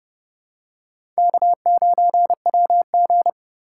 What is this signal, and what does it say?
Morse code, heard as a keyed tone, about 30 wpm, 700 Hz. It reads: K9WG
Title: USA Call Signs - 4 Characters - 3x Character Spacing - 30wpm